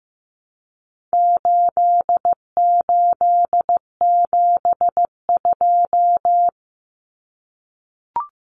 Morse code, heard as a keyed tone, about 15 wpm, 700 Hz. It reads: 8872
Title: Numbers - 4-Digit 15wpm